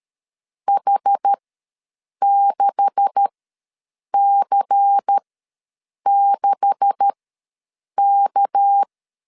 Morse code, heard as a keyed tone, about 13 wpm, spaced as for 7 wpm, 800 Hz. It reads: H6C6K
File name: LEZIONI / 9 286 ws 12wpm 3351